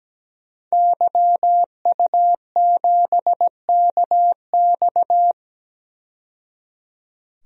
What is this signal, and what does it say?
Morse code, heard as a keyed tone, about 17 wpm, 700 Hz. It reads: YU7KX
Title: DXCC-100 Call Signs - 5 Characters - 17wpm